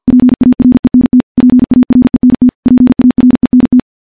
Morse code, heard as a keyed tone, about 51 wpm, 250 Hz. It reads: PARENT PARENT PARENT
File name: morse / morse-training-50WPM